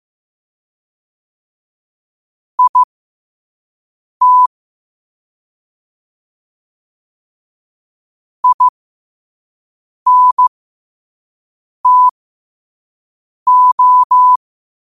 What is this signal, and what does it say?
Morse code, heard as a keyed tone, about 15 wpm, spaced as for 5 wpm, 1000 Hz. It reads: IT INTO
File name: qotdfiles / 5wpm